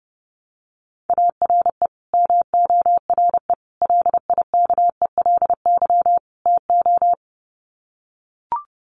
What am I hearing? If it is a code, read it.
Morse code, heard as a keyed tone, about 30 wpm, 700 Hz. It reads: ARE MORE LIKELY TO
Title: Sets of 4 Words 30wpm